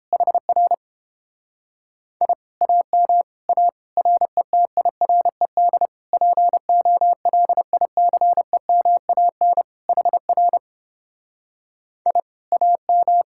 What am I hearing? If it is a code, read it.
Morse code, heard as a keyed tone, about 29 wpm, 700 Hz. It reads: HR I AM A RETIRED POLICEMAN HR I AM